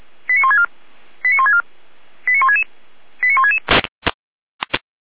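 Five-tone selective call (ZVEI), heard as quick runs of five tones, tones of about 80 ms. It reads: 89164 89164 89166 89166
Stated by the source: ZVEI five-tone selective call received by SDR, fire brigade Feuerwehr Orlamünde.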